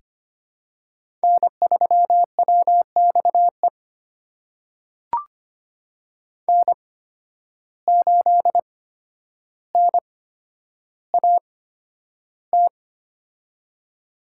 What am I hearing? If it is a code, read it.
Morse code, heard as a keyed tone, about 25 wpm, spaced as for 11 wpm, 700 Hz. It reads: N3WXE N8NAT
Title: USA Call Signs - 8x Character Spacing - 25wpm